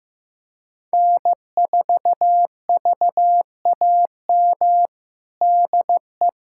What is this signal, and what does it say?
Morse code, heard as a keyed tone, about 16 wpm, 700 Hz. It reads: N4VAM DE
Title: USA Callsign DE Callsign - 5 Characters - 15wpm